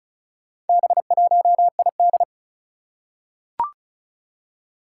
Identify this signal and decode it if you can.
Morse code, heard as a keyed tone, about 36 wpm, 700 Hz. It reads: B1ID
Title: Four Challenging Letter-Number Combinations 35wpm